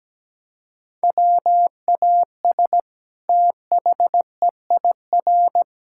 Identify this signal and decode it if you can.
Morse code, heard as a keyed tone, about 17 wpm, 700 Hz. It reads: WAS THEIR
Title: Sentences from Top 1000 Words - Review Part 6 - 17wpm